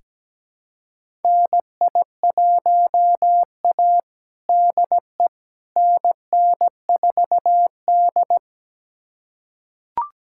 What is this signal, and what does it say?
Morse code, heard as a keyed tone, about 17 wpm, 700 Hz. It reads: NI1A DE NN4D
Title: USA Callsign DE Callsign - 4 Characters - 17wpm